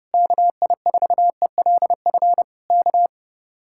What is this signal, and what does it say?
Morse code, heard as a keyed tone, about 30 wpm, 700 Hz. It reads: KI4ELF K
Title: Speed-Racing Common QSO Phrases - 30wpm